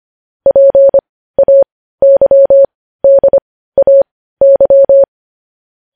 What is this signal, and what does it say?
Morse code, heard as a keyed tone, about 25 wpm, spaced as for 15 wpm, 550 Hz. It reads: PAYDAY
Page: news-headlines-in-morse-code-at-15-wpm